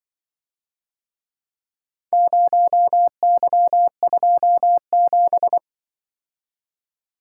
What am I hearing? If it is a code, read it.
Morse code, heard as a keyed tone, about 25 wpm, 700 Hz. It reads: 0Y27